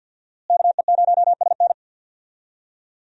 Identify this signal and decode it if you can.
Morse code, heard as a keyed tone, about 46 wpm, 700 Hz. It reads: KE0SN
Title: USA Call Signs - 5 Characters - Repeated 3x - 50wpm